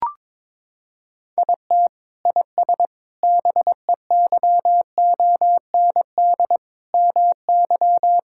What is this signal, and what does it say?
Morse code, heard as a keyed tone, about 21 wpm, 700 Hz. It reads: IT IS BEYOND MY
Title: Sentences from Top 1000 Words - Review Part 6 - 22wpm